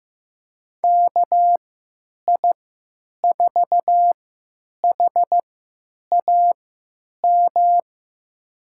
Morse code, heard as a keyed tone, about 15 wpm, spaced as for 9 wpm, 700 Hz. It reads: KI4HAM
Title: USA Call Signs - 6 Characters - 3x Character Spacing - 15wpm